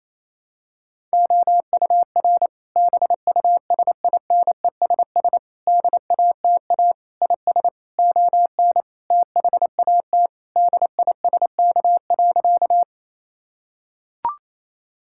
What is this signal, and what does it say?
Morse code, heard as a keyed tone, about 28 wpm, 700 Hz. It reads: OUR BUSINESS DATA IS ON THAT DISK.
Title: Sentences from Top 1000 Words - Review Part 6 - 28wpm